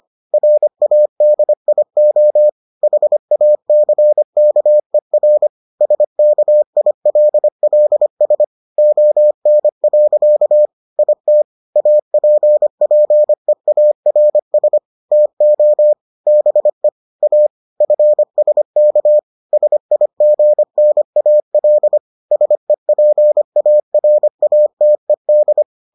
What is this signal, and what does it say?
Morse code, heard as a keyed tone, about 25 wpm, 600 Hz. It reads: RADIO HACKER SKILLS ON. IT APPEARS TO BE A FSK SIGNAL SEPARATED